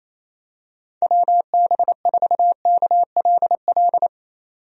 Morse code, heard as a keyed tone, about 28 wpm, 700 Hz. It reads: WB4KLL
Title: USA Call Signs - 6 Characters - 28wpm